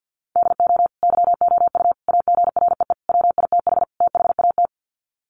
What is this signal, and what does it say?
Morse code, heard as a keyed tone, about 50 wpm, 700 Hz. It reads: DO YOU AGREE WITH THAT
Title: Sets of 5 Words Encore 50wpm